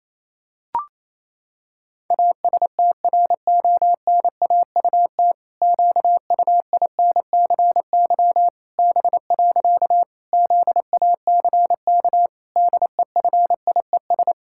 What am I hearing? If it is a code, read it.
Morse code, heard as a keyed tone, about 28 wpm, 700 Hz. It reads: ASTRONAUT QUINCY B. ZACK DEFIES
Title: Pangram Challenge v2 - 28wpm